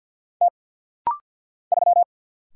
Morse code, heard as a keyed tone, about 49 wpm, 700 Hz. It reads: T 3
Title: Single Letter-Number Alt - Rapid-Fire 50wpm